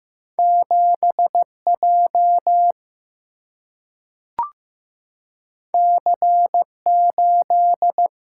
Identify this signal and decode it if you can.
Morse code, heard as a keyed tone, about 15 wpm, 700 Hz. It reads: 7 J C 8
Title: Single-Letter Single-Number 15wpm